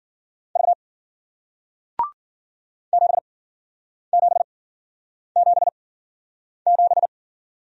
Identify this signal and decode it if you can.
Morse code, heard as a keyed tone, about 53 wpm, spaced as for 9 wpm, 700 Hz. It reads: V 7777